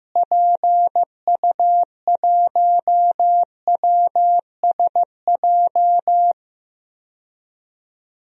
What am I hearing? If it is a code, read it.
Morse code, heard as a keyed tone, about 15 wpm, 700 Hz. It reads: PU1WSJ
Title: DXCC-100 Call Signs - 6 Characters - Repeated 3x - 15wpm